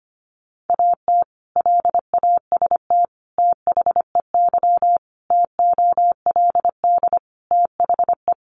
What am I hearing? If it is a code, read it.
Morse code, heard as a keyed tone, about 25 wpm, 700 Hz. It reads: AT LAST THEY TOLD THE
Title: Sentences from Top 1200 Words 25wpm